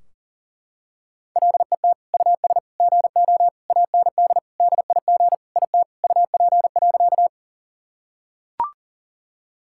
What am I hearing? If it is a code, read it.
Morse code, heard as a keyed tone, about 40 wpm, 700 Hz. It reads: LET US GO AND DIG IT UP.
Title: Sentences from Top 1600 Words 40wpm